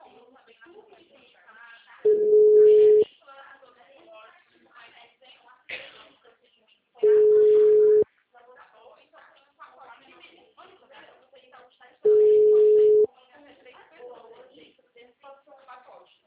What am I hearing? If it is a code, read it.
Ringback tone (Europe, 425 Hz), with speech.
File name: out- -2017- - - .41435